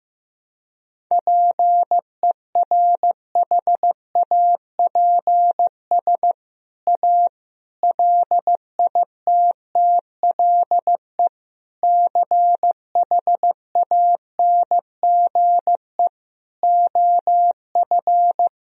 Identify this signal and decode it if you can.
Morse code, heard as a keyed tone, about 15 wpm, 700 Hz. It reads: PERHAPS A LITTLE CHANGE OF
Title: Sentences from Top 1000 Words - Review Part 1 - 15wpm